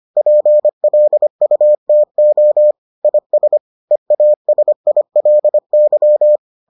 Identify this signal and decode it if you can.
Morse code, heard as a keyed tone, about 25 wpm, 600 Hz. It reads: PLUTO IS EASILY